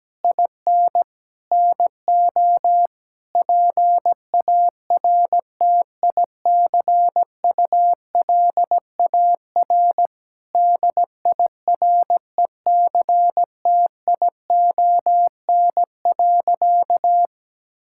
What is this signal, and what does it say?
Morse code, heard as a keyed tone, about 17 wpm, 700 Hz. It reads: IN NO PARTICULAR DIRECTION.
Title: Sentences from Top 1000 Words - Review Part 5 - 17wpm